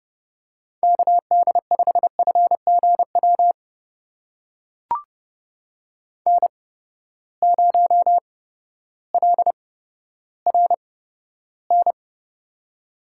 Morse code, heard as a keyed tone, about 30 wpm, spaced as for 14 wpm, 700 Hz. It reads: KD5FGW N0LRN
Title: USA Call Signs - 8x Character Spacing - 30wpm